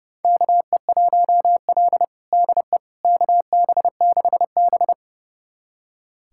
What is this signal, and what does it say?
Morse code, heard as a keyed tone, about 30 wpm, 700 Hz. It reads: KE1L DE KB6B
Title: USA Callsign DE Callsign - 4 Characters - 30wpm